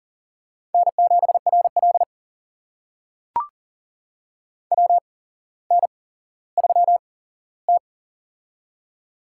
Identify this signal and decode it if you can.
Morse code, heard as a keyed tone, about 40 wpm, spaced as for 17 wpm, 700 Hz. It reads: N7RL WN3T
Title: USA Call Signs - 4 Characters - 8x Character Spacing - 40wpm